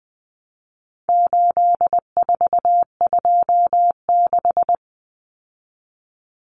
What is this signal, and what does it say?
Morse code, heard as a keyed tone, about 20 wpm, 700 Hz. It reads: 8426
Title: Numbers - 4-Digit 20wpm